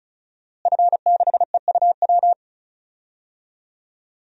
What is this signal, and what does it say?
Morse code, heard as a keyed tone, about 34 wpm, 700 Hz. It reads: F6EUW
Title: DXCC-100 Call Signs - 5 Characters - Repeated 2x - 35wpm